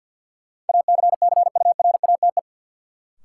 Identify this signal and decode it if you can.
Morse code, heard as a keyed tone, about 50 wpm, 700 Hz. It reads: ACCURATE
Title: Sentences from Top 1200 Words - New Word Drill Alt - 50wpm